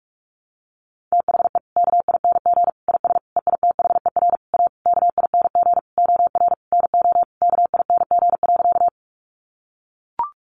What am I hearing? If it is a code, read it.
Morse code, heard as a keyed tone, about 44 wpm, 700 Hz. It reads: THE KING IS EITHER A KING OR NO KING.
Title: Sentences from Top 600 Words 45wpm